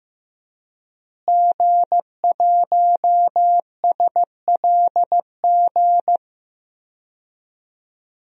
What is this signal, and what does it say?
Morse code, heard as a keyed tone, about 15 wpm, 700 Hz. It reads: G1SLG
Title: DXCC-100 Call Signs - 5 Characters - Repeated 2x - 15wpm